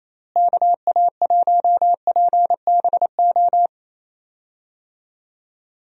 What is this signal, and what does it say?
Morse code, heard as a keyed tone, about 28 wpm, 700 Hz. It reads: KA1PBO
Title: USA Call Signs - 6 Characters - Repeated 2x - 28wpm